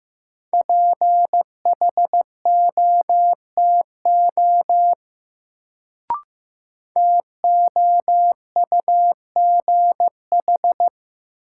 Morse code, heard as a keyed tone, about 15 wpm, 700 Hz. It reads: PHOTO TOUGH
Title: Sentences from Top 1200 Words - New Word Drill - 15wpm